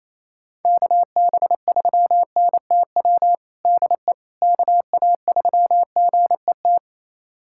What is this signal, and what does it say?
Morse code, heard as a keyed tone, about 28 wpm, 700 Hz. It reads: KB3NTW DE KA3GET
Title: USA Callsign DE Callsign - 6 Characters - 28wpm